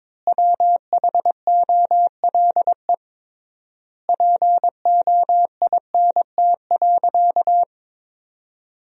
Morse code, heard as a keyed tone, about 22 wpm, 700 Hz. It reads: WHOLE POINT.
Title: 3x Word-Spacing - Sentences from Top 400 Words - 22wpm